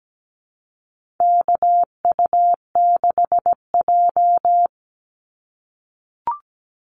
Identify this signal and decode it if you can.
Morse code, heard as a keyed tone, about 17 wpm, 700 Hz. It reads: KU6J
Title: NA SOTA Activator Callsigns 17wpm